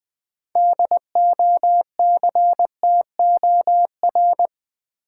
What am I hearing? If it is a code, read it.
Morse code, heard as a keyed tone, about 21 wpm, 700 Hz. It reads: DOCTOR
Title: Sentences from Top 1500 Words - New Word Drill - 20wpm